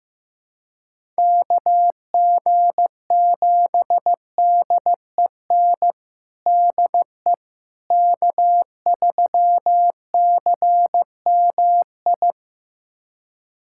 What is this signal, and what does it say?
Morse code, heard as a keyed tone, about 15 wpm, 700 Hz. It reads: KG7DEN DE K3CMI
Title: USA Callsign DE Callsign 15wpm